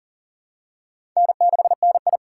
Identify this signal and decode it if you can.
Morse code, heard as a keyed tone, about 40 wpm, 700 Hz. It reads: N6NI
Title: USA Call Signs - 4x Character Spacing - 40wpm